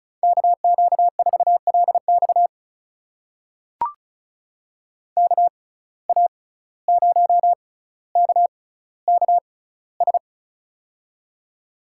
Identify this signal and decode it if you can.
Morse code, heard as a keyed tone, about 36 wpm, spaced as for 17 wpm, 700 Hz. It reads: KQ4LX KA0KKS
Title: USA Call Signs - 6x Character Spacing - 35wpm